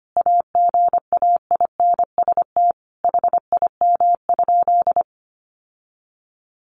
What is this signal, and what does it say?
Morse code, heard as a keyed tone, about 25 wpm, 700 Hz. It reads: AGAINST HIM?
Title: Sentences from Top 1000 Words 25wpm